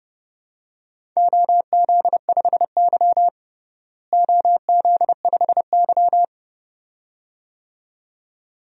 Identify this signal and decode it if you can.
Morse code, heard as a keyed tone, about 29 wpm, 700 Hz. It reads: OZ5Y OZ5Y
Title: DXCC-100 Call Signs - 4 Characters - Repeated 2x - 30wpm